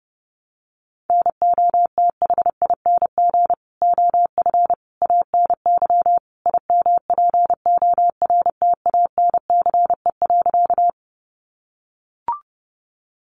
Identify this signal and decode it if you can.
Morse code, heard as a keyed tone, about 30 wpm, 700 Hz. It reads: NOTHING OF ANY IMPORTANCE.
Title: Sentences from Top 1200 Words 30wpm